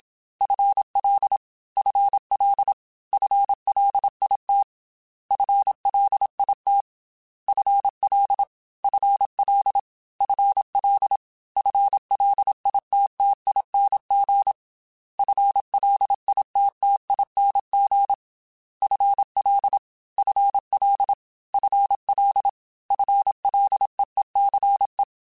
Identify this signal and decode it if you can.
Morse code, heard as a keyed tone, about 27 wpm, 800 Hz. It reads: FL FL FLIT FLIT FL FL FL FLITTING FLITTING FL FL FL FLEECE
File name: FL / FL mp3s / FL Prefixes 3x2 @25